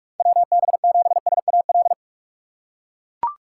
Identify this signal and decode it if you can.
Morse code, heard as a keyed tone, about 41 wpm, 700 Hz. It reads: WB7SAL
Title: USA Call Signs - 6 Characters - 6x Character Spacing - 45wpm